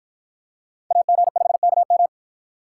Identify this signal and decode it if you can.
Morse code, heard as a keyed tone, about 50 wpm, 700 Hz. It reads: AG5KM